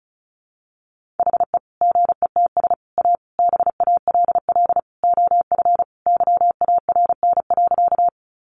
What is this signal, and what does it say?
Morse code, heard as a keyed tone, about 34 wpm, 700 Hz. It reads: HE GETS A BALL OF YARN.
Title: Additional Easy Phrases 35wpm